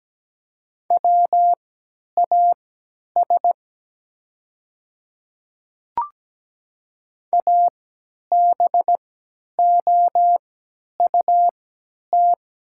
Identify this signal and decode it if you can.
Morse code, heard as a keyed tone, about 17 wpm, spaced as for 10 wpm, 700 Hz. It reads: WAS ABOUT
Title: 3x Character Spacing - Top 200 Words - 17wpm